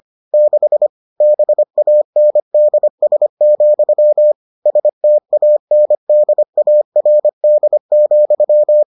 Morse code, heard as a keyed tone, about 25 wpm, 600 Hz. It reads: 6 BANDS, STANDARD,